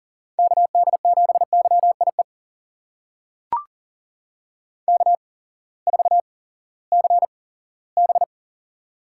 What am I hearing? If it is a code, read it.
Morse code, heard as a keyed tone, about 42 wpm, spaced as for 20 wpm, 700 Hz. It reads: KD7YIE K4CB
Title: USA Call Signs - 8x Character Spacing - 40wpm